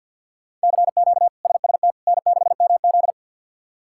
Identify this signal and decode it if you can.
Morse code, heard as a keyed tone, about 52 wpm, 700 Hz. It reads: CQ SST N6MZ
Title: K1USN SST QSO Practice - 50wpm